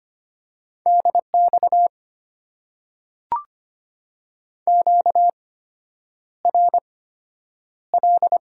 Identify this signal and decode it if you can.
Morse code, heard as a keyed tone, about 25 wpm, spaced as for 9 wpm, 700 Hz. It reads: DX QRL
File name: Top 90 QSO Elements - 8x Character Spacing - 25wpm